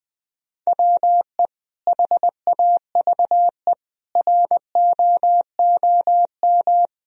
Morse code, heard as a keyed tone, about 20 wpm, 700 Hz. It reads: WE HAVE ROOM
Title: Sentences from Top 1000 Words - Review Part 1 - 20wpm